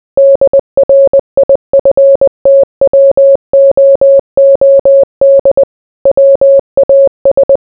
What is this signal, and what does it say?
Morse code, heard as a keyed tone, about 20 wpm, 550 Hz. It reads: DRIFTWOOD WAS